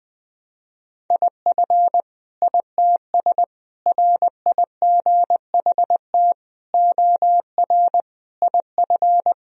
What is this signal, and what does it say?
Morse code, heard as a keyed tone, about 20 wpm, 700 Hz. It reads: IF ITS RIGHT OR IF
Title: Sentences from Top 1000 Words - Review Part 3 - 20wpm